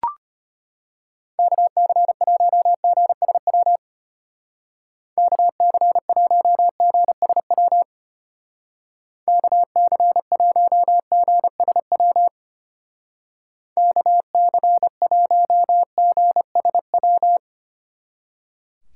Morse code, heard as a keyed tone, about 31 wpm, 700 Hz. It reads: KC1GSW KC1GSW KC1GSW KC1GSW
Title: Speed Racing - USA Call Signs 25wpm